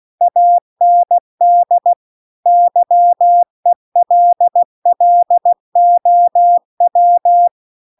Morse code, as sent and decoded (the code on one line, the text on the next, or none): .- -. -.. / -.-- . .-.. .-.. --- .--
AND YELLOW